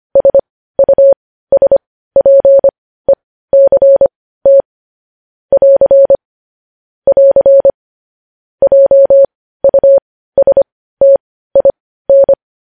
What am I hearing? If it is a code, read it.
Morse code, heard as a keyed tone, about 25 wpm, spaced as for 15 wpm, 550 Hz. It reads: SUSPECT <AR> <AR> JUSTIN
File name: podcast / News headlines-1x-25x15